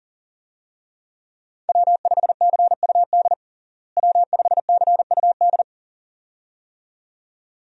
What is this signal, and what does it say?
Morse code, heard as a keyed tone, about 40 wpm, 700 Hz. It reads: W5CUD W5CUD
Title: USA Call Signs - 5 Characters - Repeated 2x - 40wpm